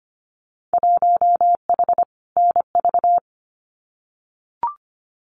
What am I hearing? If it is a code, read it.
Morse code, heard as a keyed tone, about 25 wpm, 700 Hz. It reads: 1H NV
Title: Winter Field Day Exchanges 25wpm